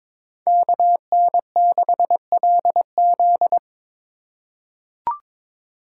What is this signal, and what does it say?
Morse code, heard as a keyed tone, about 22 wpm, 700 Hz. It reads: KN6LZ
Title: USA Call Signs - 5 Characters - 4x Character Spacing - 22wpm